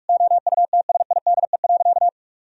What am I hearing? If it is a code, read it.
Morse code, heard as a keyed tone, about 46 wpm, 700 Hz. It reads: OUTSIDE.
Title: Sentences from Top 1000 Words - Review Part 4 - 45wpm